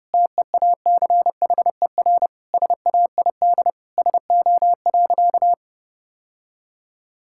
Morse code, heard as a keyed tone, about 32 wpm, 700 Hz. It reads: TEACHER SAID SO.
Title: Sentences from Top 1000 Words - Review Part 6 - 30wpm